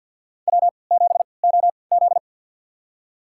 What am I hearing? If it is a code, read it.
Morse code, heard as a keyed tone, about 48 wpm, spaced as for 28 wpm, 700 Hz. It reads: W7OZ
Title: USA Call Signs - 4 Characters - 3x Character Spacing - 50wpm